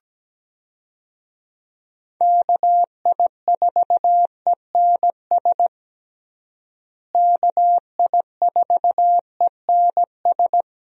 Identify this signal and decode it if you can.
Morse code, heard as a keyed tone, about 17 wpm, 700 Hz. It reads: KI4ENS KI4ENS
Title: USA Call Signs - 6 Characters - Repeated 3x - 17wpm